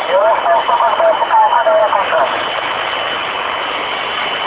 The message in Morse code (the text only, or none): none